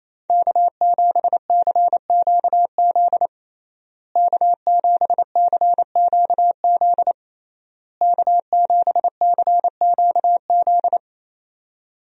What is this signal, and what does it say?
Morse code, heard as a keyed tone, about 28 wpm, 700 Hz. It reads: K7CQZ K7CQZ K7CQZ
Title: USA Call Signs - 5 Characters - Repeated 3x - 28wpm